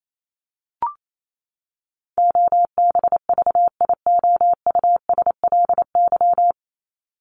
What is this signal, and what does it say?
Morse code, heard as a keyed tone, about 28 wpm, 700 Hz. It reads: OBVIOUSLY
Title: LY Suffix - Top 100 Words with Base Words 28wpm